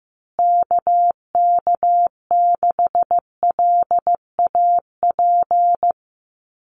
Morse code, heard as a keyed tone, about 15 wpm, 700 Hz. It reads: KK6LAP
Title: USA Call Signs Repeated 2x 15wpm